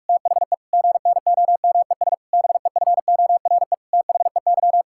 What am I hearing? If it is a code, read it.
Morse code, heard as a keyed tone, about 41 wpm, 700 Hz. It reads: THE GNOMES BEFORE THEY
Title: Pangram Challenge v2 - 45wpm